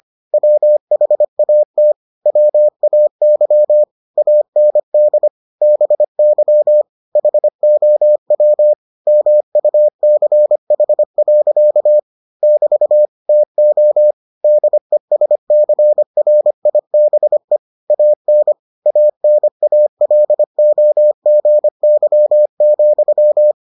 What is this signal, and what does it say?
Morse code, heard as a keyed tone, about 25 wpm, 600 Hz. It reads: WHAT WAY AND BY HOW MUCH. <BT> TO DESCRIBE AN ANALOGY,